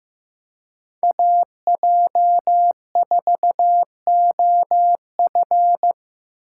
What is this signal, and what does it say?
Morse code, heard as a keyed tone, about 15 wpm, 700 Hz. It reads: AJ4OF
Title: USA Call Signs - 5 Characters - 3x Character Spacing - 15wpm